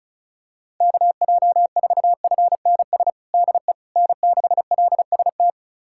Morse code, heard as a keyed tone, about 34 wpm, 700 Hz. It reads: KJ4FNS DE N6LST
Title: USA Callsign DE Callsign 35wpm